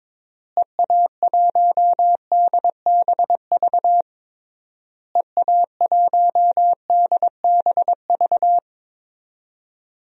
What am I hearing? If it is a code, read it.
Morse code, heard as a keyed tone, about 22 wpm, 700 Hz. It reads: EA1DBV EA1DBV
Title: DXCC-100 Call Signs - 6 Characters - Repeated 3x - 22wpm